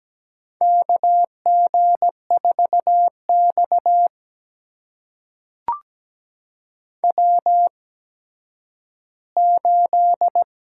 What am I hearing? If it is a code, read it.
Morse code, heard as a keyed tone, about 17 wpm, spaced as for 10 wpm, 700 Hz. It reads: KG4X W8
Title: USA Call Signs - 4 Characters - 8x Character Spacing - 17wpm